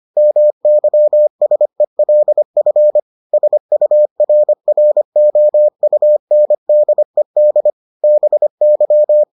Morse code, heard as a keyed tone, about 25 wpm, 600 Hz. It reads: MYSELF SURROUNDED BY